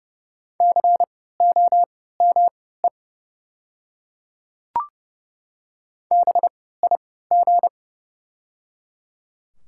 Morse code, heard as a keyed tone, about 30 wpm, spaced as for 17 wpm, 690 Hz. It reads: COME BIG
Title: 3x Character Spacing - Top 500 Words - 30wpm